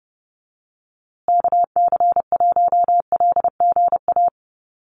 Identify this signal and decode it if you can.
Morse code, heard as a keyed tone, about 30 wpm, 700 Hz. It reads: KC1LGA
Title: USA Call Signs Repeated 3x 30wpm